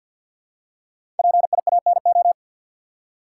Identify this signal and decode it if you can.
Morse code, heard as a keyed tone, about 46 wpm, 700 Hz. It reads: PIANO
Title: Five Letter Words - 50wpm